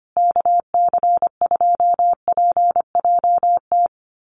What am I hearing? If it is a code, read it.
Morse code, heard as a keyed tone, about 25 wpm, 700 Hz. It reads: KC2PJT
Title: USA Call Signs Repeated 2x 25wpm